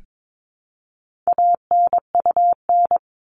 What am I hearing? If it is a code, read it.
Morse code, heard as a keyed tone, about 22 wpm, 700 Hz. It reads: ANUN
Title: All Cut Numbers - 4-Digit 22wpm